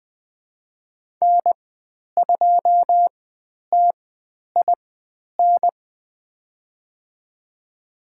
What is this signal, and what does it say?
Morse code, heard as a keyed tone, about 20 wpm, spaced as for 10 wpm, 700 Hz. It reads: N2TIN
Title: 109 Lesson 18g - Letter m - Call Signs 20wpm